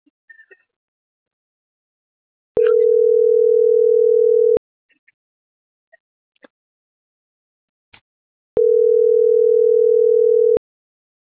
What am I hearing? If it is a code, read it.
Ringback tone (North America), 440 + 480 Hz.